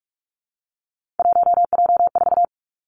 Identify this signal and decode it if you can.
Morse code, heard as a keyed tone, about 46 wpm, 700 Hz. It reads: 1J4
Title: Three Challenging Letter-Number Combinations 45wpm